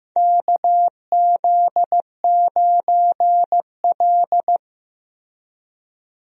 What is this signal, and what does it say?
Morse code, heard as a keyed tone, about 15 wpm, 700 Hz. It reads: KZ9L
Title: USA Call Signs - 4 Characters - Repeated 3x - 15wpm